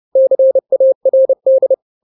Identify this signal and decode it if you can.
Morse code, heard as a keyed tone, about 30 wpm, 530 Hz. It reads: CARD
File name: daily / 29wpm